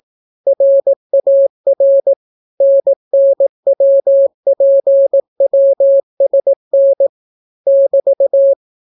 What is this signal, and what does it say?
Morse code, heard as a keyed tone, about 18 wpm, 550 Hz. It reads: RAR NNWPWSN <BT>